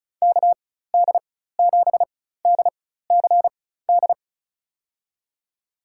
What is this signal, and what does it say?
Morse code, heard as a keyed tone, about 34 wpm, spaced as for 16 wpm, 700 Hz. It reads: KD7DCD
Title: USA Call Signs - 4x Character Spacing - 35wpm